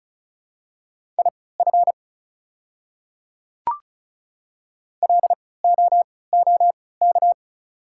Morse code, heard as a keyed tone, about 36 wpm, spaced as for 20 wpm, 690 Hz. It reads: IF LOOK
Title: 3x Character Spacing - Top 100 Words - 35wpm